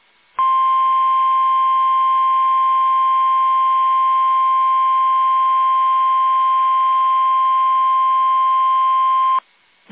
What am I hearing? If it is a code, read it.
Emergency Alert System (SAME) tones (the 1050 Hz attention tone). It is heard from 0.5 to 9.3 s.
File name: eas / KUSO / KUSO